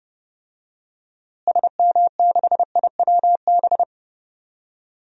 Morse code, heard as a keyed tone, about 30 wpm, 700 Hz. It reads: SM6IWB
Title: DXCC-100 Call Signs - 6 Characters - Repeated 2x - 30wpm